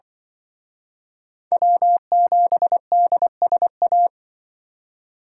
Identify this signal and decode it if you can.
Morse code, heard as a keyed tone, about 23 wpm, 700 Hz. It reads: W7DSA
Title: Speed Racing - USA 1x3 Call Signs 20wpm